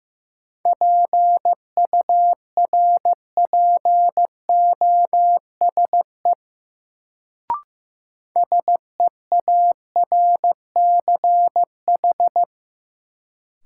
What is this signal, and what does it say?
Morse code, heard as a keyed tone, about 15 wpm, 700 Hz. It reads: PURPOSE SEARCH
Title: Sentences from Top 800 Words - New Word Drill - 15wpm